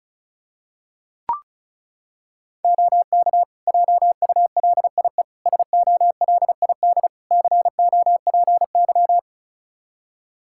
Morse code, heard as a keyed tone, about 36 wpm, 700 Hz. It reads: OK JULIE SOLID COPY
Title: Practice QSO - Exchange 3 - More Info - 35wpm